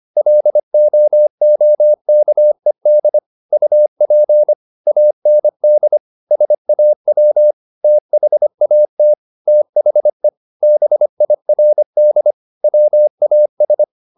Morse code, heard as a keyed tone, about 25 wpm, 600 Hz. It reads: LOOKED UP AND SAW THAT THE BIRD WAS
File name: the old man and the sea part 1 25wpm